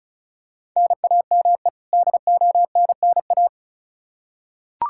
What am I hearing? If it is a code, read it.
Morse code, heard as a keyed tone, about 34 wpm, 700 Hz. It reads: NAME DONNA
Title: Practice QSO - Exchange 1 - Beginning - 35wpm